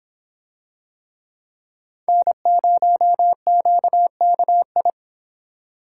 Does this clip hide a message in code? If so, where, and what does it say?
Yes: Morse code from 2.1 to 4.9 s, heard as a keyed tone, about 26 wpm, 700 Hz. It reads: N0QKI